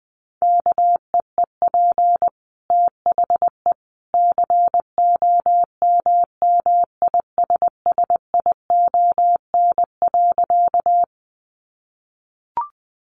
Morse code, heard as a keyed tone, about 20 wpm, 700 Hz. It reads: KEEP THE COMMISSION.
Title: Sentences from Top 1400 Words 20wpm